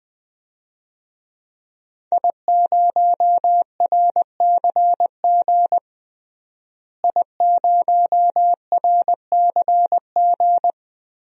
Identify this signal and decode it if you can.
Morse code, heard as a keyed tone, about 20 wpm, 700 Hz. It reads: I0RCG I0RCG
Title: DXCC-100 Call Signs - 5 Characters - Repeated 2x - 20wpm